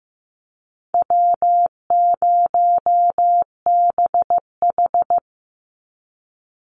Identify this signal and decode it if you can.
Morse code, heard as a keyed tone, about 15 wpm, 700 Hz. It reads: W0BH
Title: Easy Call Signs Repeated 2x - 15wpm